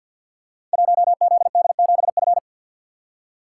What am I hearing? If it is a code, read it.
Morse code, heard as a keyed tone, about 52 wpm, 700 Hz. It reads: 1ZD7F
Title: Code Groups - 5 Letter:Numbers - 50wpm